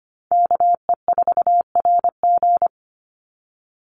Morse code, heard as a keyed tone, about 25 wpm, 700 Hz. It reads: KE4RG
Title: Call Signs - 25wpm